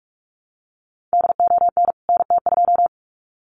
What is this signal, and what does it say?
Morse code, heard as a keyed tone, about 44 wpm, 700 Hz. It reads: DON NT2
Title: CWT Exchange 45wpm